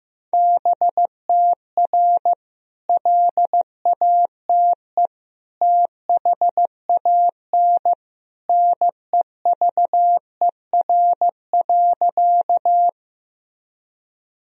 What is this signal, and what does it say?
Morse code, heard as a keyed tone, about 16 wpm, 700 Hz. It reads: BTR LATE THAN NEVER.
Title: Sentences from Top 400 Words and Abbreviations 15wpm